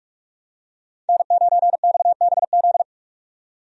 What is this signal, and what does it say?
Morse code, heard as a keyed tone, about 42 wpm, 700 Hz. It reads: N9XBZ
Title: USA Call Signs - 5 Characters - 45wpm